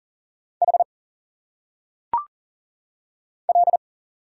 Morse code, heard as a keyed tone, about 40 wpm, 700 Hz. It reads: H L
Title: Single Letters - Phonetic Pronunciation 40wpm